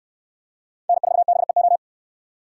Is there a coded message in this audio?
Yes: Morse code from 0.9 to 1.8 s, heard as a keyed tone, about 62 wpm, 700 Hz. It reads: N4BEO